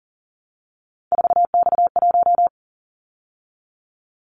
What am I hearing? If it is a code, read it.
Morse code, heard as a keyed tone, about 40 wpm, 700 Hz. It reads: 4X1
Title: Callsign Prefixes for Top 50 DXCC - 40wpm